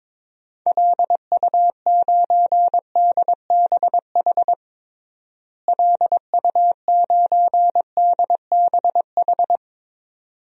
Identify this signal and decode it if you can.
Morse code, heard as a keyed tone, about 22 wpm, 700 Hz. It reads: LU9DBH LU9DBH
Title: DXCC-100 Call Signs - 6 Characters - Repeated 2x - 22wpm